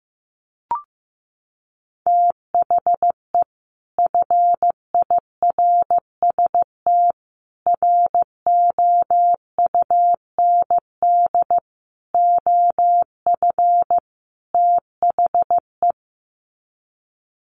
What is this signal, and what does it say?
Morse code, heard as a keyed tone, about 15 wpm, 700 Hz. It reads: THE FIRST ROUND OF THE
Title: Sets of 5 Words Encore 15wpm